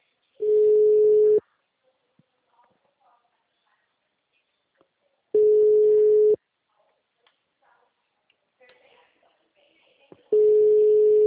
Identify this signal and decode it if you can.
Ringback tone (Europe, 425 Hz), with speech.
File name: out- -2030- - - .1901